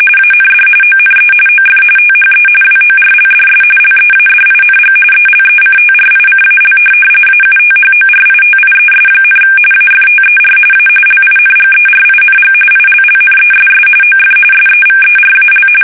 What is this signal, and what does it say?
RTTY, about 45.45 baud. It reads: AND ONE OF THE WORLDS LEADING ELECTRICAL AUTHORITIES, AFTER A LINGERING ILLNESS OF SEVERAL
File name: fsk850 / benjamin-g-lamme